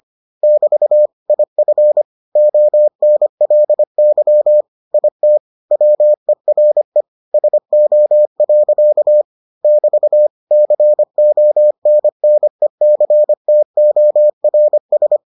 Morse code, heard as a keyed tone, about 25 wpm, 600 Hz. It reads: <BT> IF ONLY IT WERE SO. <BT> CONNECTORS